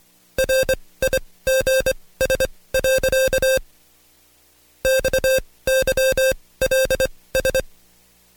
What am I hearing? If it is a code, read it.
Morse code, heard as a keyed tone, about 23 wpm, spaced as for 17 wpm, 530 Hz. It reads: RIGS. XYLS
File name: Code Quick / AudioDownld2 / Seg135-Stepper8 17WPM